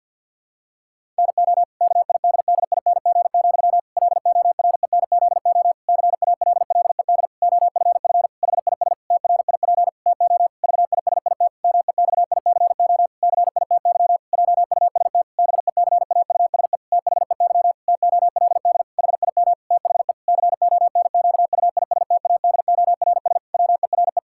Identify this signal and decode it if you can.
Morse code, heard as a keyed tone, about 50 wpm, 700 Hz. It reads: NO KIDDING, LORENZO CALLED OFF HIS TRIP TO VISIT MEXICO CITY JUST BECAUSE THEY TOLD HIM THE CONQUISTADORS WERE
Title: Pangram Challenge v2 - 50wpm